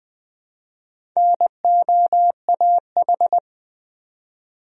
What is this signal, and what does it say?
Morse code, heard as a keyed tone, about 21 wpm, 700 Hz. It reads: NOAH
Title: USA Common Names - Max 5 Letters - 20wpm